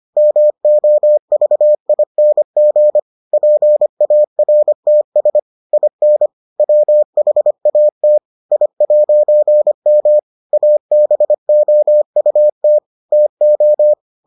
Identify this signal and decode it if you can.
Morse code, heard as a keyed tone, about 25 wpm, 600 Hz. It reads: MOVING PARTS IN WHAT I'M ABOUT TO